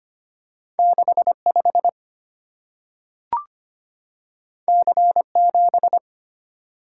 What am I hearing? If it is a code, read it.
Morse code, heard as a keyed tone, about 25 wpm, 700 Hz. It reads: 65 C7
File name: Two Letter-Number Combinations - 25wpm